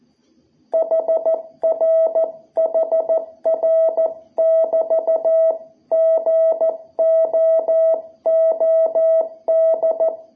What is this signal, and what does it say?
Morse code, heard as a keyed tone, about 14 wpm, 660 Hz. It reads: HR HR <BT> GOOD
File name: jh8cbh / 2021cw